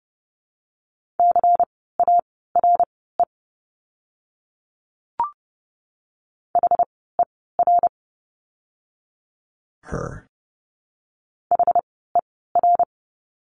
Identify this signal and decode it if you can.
Morse code, heard as a keyed tone, about 30 wpm, spaced as for 17 wpm, 700 Hz. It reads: CARE HER HER
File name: Lesson 17f - Letter u - Words 30wpm